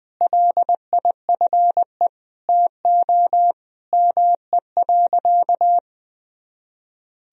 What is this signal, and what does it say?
Morse code, heard as a keyed tone, about 21 wpm, 700 Hz. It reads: LIFE TO ME.
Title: Sentences from Top 500 Words and Abbreviations 20wpm